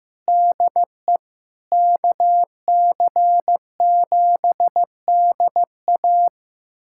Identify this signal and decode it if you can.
Morse code, heard as a keyed tone, about 16 wpm, 700 Hz. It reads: DE KC7DA
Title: USA Callsign DE Callsign - 5 Characters - 15wpm